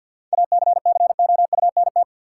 Morse code, heard as a keyed tone, about 53 wpm, 700 Hz. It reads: ACCOUNT
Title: Sentences from Top 900 Words - New Word Drill - 50wpm